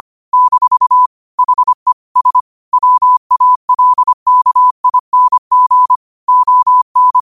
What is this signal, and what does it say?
Morse code, heard as a keyed tone, about 25 wpm, 1000 Hz. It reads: - HE S WALKING ON
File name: qotdfiles / 25wpm